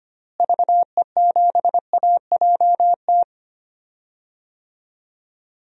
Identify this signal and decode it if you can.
Morse code, heard as a keyed tone, about 25 wpm, 700 Hz. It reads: VE7AJT
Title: DXCC-100 Call Signs - 6 Characters - Repeated 2x - 25wpm